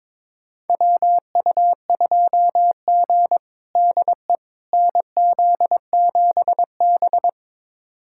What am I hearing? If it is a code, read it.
Morse code, heard as a keyed tone, about 22 wpm, 700 Hz. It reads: WU2G DE NZ7B
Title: USA Callsign DE Callsign - 4 Characters - 22wpm